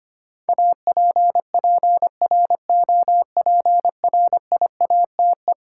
Morse code, heard as a keyed tone, about 25 wpm, 700 Hz. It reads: APPROPRIATE
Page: morse-code-ninja-practice-25wpm